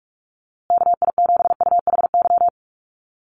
Morse code, heard as a keyed tone, about 44 wpm, 700 Hz. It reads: KI7UHY
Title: USA Call Signs Repeated 2x 45wpm